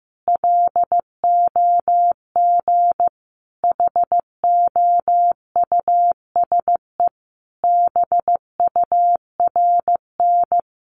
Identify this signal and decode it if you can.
Morse code, heard as a keyed tone, about 15 wpm, 700 Hz. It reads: LOG HOUSE BURN
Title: Sentences from Top 1400 Words 15wpm